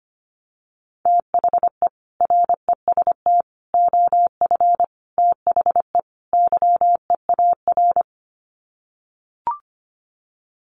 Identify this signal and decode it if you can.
Morse code, heard as a keyed tone, about 25 wpm, 690 Hz. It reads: THE REST OF THE YEAR
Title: Sets of 5 Words Encore 25wpm